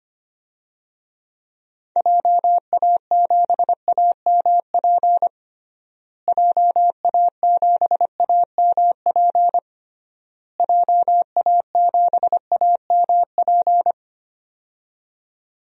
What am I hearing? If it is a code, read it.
Morse code, heard as a keyed tone, about 25 wpm, 700 Hz. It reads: JA7AMP JA7AMP JA7AMP
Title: DXCC-100 Call Signs - 6 Characters - Repeated 3x - 25wpm